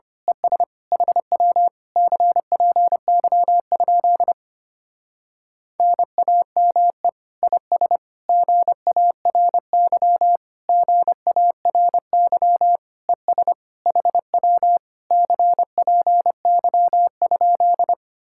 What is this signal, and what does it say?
Morse code, heard as a keyed tone, about 25 wpm, 690 Hz. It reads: ES HW CPY? NAME IS GARY GARY ES HW CPY?
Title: Speed-Racing Common QSO Phrases - 25wpm